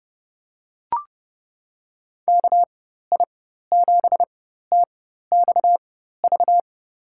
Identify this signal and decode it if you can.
Morse code, heard as a keyed tone, about 30 wpm, spaced as for 14 wpm, 700 Hz. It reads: KI7TXV
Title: USA Call Signs - 4x Character Spacing - 30wpm